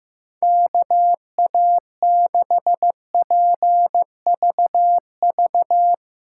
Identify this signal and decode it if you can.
Morse code, heard as a keyed tone, about 15 wpm, 700 Hz. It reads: KA6PVV
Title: USA Call Signs - 6 Characters - 6x Character Spacing - 15wpm